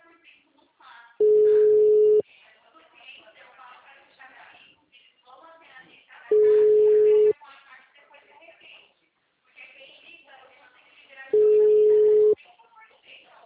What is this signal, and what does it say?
Ringback tone (Europe, 425 Hz), with speech.